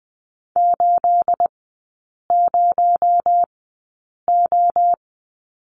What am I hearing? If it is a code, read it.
Morse code, heard as a keyed tone, about 19 wpm, spaced as for 8 wpm, 700 Hz. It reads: 80O
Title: 218 Lesson 35e - Number 0 - Three Characters 20wpm